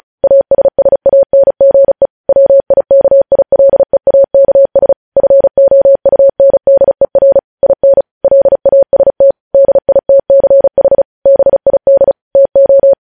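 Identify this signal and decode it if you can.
Morse code, heard as a keyed tone, about 36 wpm, 550 Hz. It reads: ASSANGE WIKILEAKS FOUNDER IN LAST DITCH BID TO
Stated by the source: Morse news headlines practice at 35.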